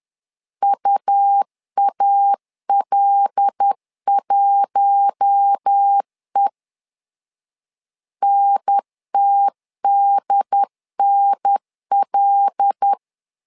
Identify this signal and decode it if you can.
Morse code, heard as a keyed tone, about 10 wpm, 800 Hz. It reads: UAL1E NTDNL
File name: LEZIONI / 5 LBVC / 5 L 10wpm 2259